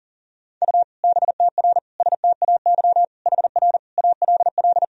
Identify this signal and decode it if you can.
Morse code, heard as a keyed tone, about 40 wpm, 700 Hz. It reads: U BTR STAY HR ALL
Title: Sentences from Top 500 Words and Abbreviations 40wpm